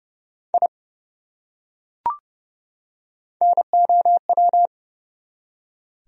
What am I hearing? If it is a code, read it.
Morse code, heard as a keyed tone, about 29 wpm, 700 Hz. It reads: I NOW
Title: Top 100 Words 30wpm